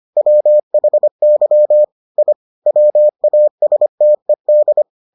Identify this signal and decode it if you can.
Morse code, heard as a keyed tone, about 25 wpm, 600 Hz. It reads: WHY I WASTED